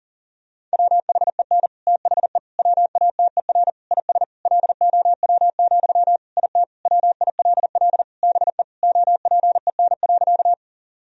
Morse code, heard as a keyed tone, about 40 wpm, 700 Hz. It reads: WHEN THE WATER IS LOW, IT WILL BE OPEN.
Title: Sentences from Top 1000 Words - Review Part 1 - 40wpm